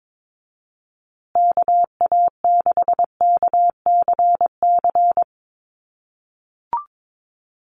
Morse code, heard as a keyed tone, about 22 wpm, 700 Hz. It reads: KA6KCC
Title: USA Call Signs Repeated 2x 22wpm